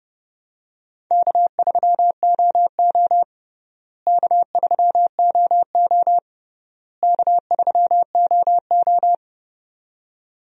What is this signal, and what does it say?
Morse code, heard as a keyed tone, about 30 wpm, 700 Hz. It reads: K3OO K3OO K3OO
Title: USA Call Signs - 4 Characters - Repeated 3x - 30wpm